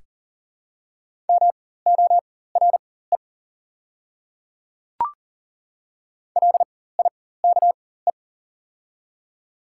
Morse code, heard as a keyed tone, about 40 wpm, spaced as for 19 wpm, 690 Hz. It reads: MORE LIKE
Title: 4x Character Spacing - Top 100 Words - 40wpm